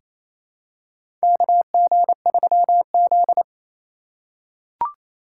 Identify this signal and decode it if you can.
Morse code, heard as a keyed tone, about 28 wpm, 700 Hz. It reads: KG3Z
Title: USA Call Signs - 4 Characters - 8x Character Spacing - 28wpm